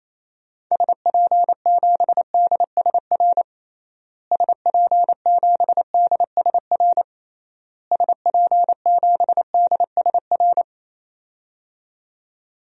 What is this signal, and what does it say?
Morse code, heard as a keyed tone, about 28 wpm, 700 Hz. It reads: SP7DSR SP7DSR SP7DSR
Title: DXCC-100 Call Signs - 6 Characters - Repeated 3x - 28wpm